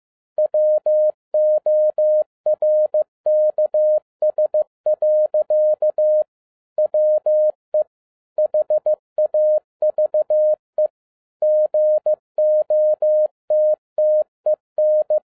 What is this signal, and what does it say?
Morse code, heard as a keyed tone, about 15 wpm, 610 Hz. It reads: WORKS. WE HAVE GOTTEN
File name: 2012-01-mp3 / n13-A NEWLY DESIGNED QSL CARD-15wpm